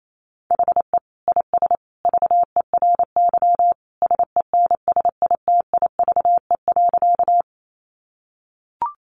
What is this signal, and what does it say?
Morse code, heard as a keyed tone, about 28 wpm, 700 Hz. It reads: HE IS VERY SENSITIVE.
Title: Sentences from Top 1400 Words 28wpm